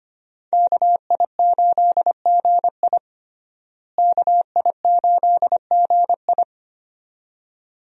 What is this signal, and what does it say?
Morse code, heard as a keyed tone, about 25 wpm, 690 Hz. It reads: KI8GI KI8GI
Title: USA Call Signs - 5 Characters - Repeated 2x - 25wpm